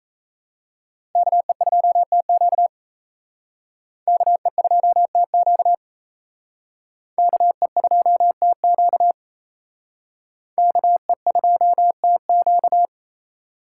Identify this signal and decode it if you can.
Morse code, heard as a keyed tone, about 35 wpm, 700 Hz. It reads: KE2TQ KE2TQ KE2TQ KE2TQ
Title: Speed Racing - USA Call Signs 28wpm